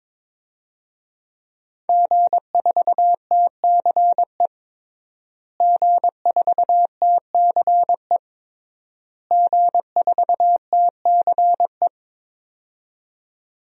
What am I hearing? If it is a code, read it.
Morse code, heard as a keyed tone, about 22 wpm, 700 Hz. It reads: G4TCE G4TCE G4TCE
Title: DXCC-100 Call Signs - 5 Characters - Repeated 3x - 22wpm